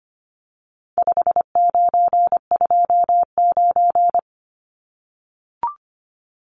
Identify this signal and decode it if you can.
Morse code, heard as a keyed tone, about 25 wpm, 700 Hz. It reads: 5929
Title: Numbers - 4-Digit 25wpm